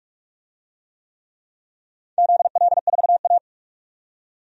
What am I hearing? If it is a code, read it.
Morse code, heard as a keyed tone, about 46 wpm, 700 Hz. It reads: ZL4A
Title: DXCC-100 Call Signs - 4 Characters - 45wpm